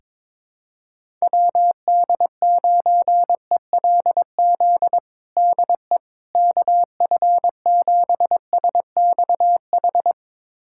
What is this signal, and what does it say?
Morse code, heard as a keyed tone, about 22 wpm, 700 Hz. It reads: WD9ELZ DE KF7SXH
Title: USA Callsign DE Callsign - 6 Characters - 22wpm